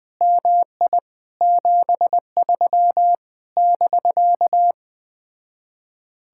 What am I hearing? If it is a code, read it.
Morse code, heard as a keyed tone, about 20 wpm, 700 Hz. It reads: MI 73 <BK>
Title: Speed-Racing Common QSO Phrases - 17wpm